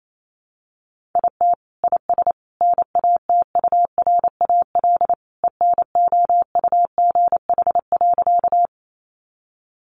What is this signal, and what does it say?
Morse code, heard as a keyed tone, about 28 wpm, 700 Hz. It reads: IT IS NATURAL ENOUGH.
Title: Sentences from Top 700 Words 28wpm